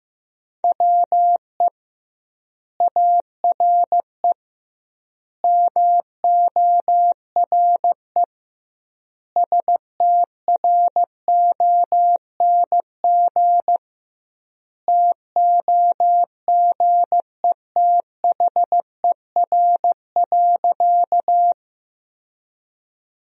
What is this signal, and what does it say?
Morse code, heard as a keyed tone, about 15 wpm, 700 Hz. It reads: WE ARE MORE STRONG TOGETHER.
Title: 2x Word-Spacing - Sentences from Top 500 Words - 15wpm